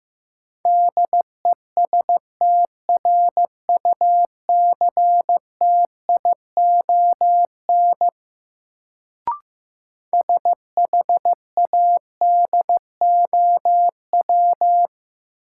Morse code, heard as a keyed tone, about 16 wpm, 700 Hz. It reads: DESTRUCTION SHADOW
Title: Sets of 5 Words Encore - New Word Drill - 15wpm